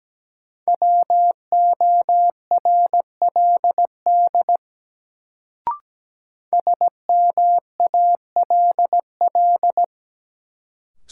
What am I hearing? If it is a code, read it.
Morse code, heard as a keyed tone, about 17 wpm, 700 Hz. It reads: WORLD SMALL
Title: Sentences from Top 200 Words - New Word Drill - 17wpm